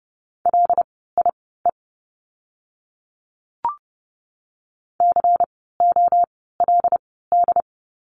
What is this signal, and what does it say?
Morse code, heard as a keyed tone, about 30 wpm, spaced as for 17 wpm, 700 Hz. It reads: LIE COLD
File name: Lesson 16f - Letter c - Words 30wpm